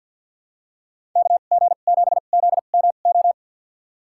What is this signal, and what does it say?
Morse code, heard as a keyed tone, about 52 wpm, spaced as for 36 wpm, 700 Hz. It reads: KG7ZMO
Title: USA Call Signs - 2x Character Spacing - 50wpm